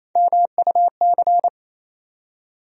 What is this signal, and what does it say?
Morse code, heard as a keyed tone, about 28 wpm, 700 Hz. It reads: MUC
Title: USA Call Sign Postfixes - 3 Letters 28wpm